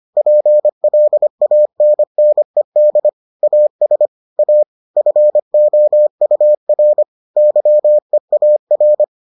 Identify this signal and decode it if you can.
Morse code, heard as a keyed tone, about 25 wpm, 600 Hz. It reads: PLANNED AS A FOUR YEAR